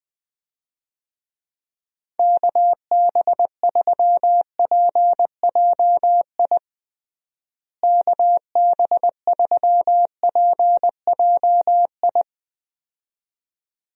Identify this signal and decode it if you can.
Morse code, heard as a keyed tone, about 21 wpm, 700 Hz. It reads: KB3PJI KB3PJI
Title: USA Call Signs - 6 Characters - Repeated 2x - 20wpm